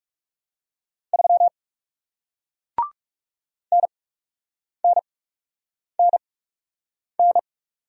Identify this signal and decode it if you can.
Morse code, heard as a keyed tone, about 37 wpm, spaced as for 8 wpm, 700 Hz. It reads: 3 NNNN
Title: Speed-Racing Single Letter-Number 30wpm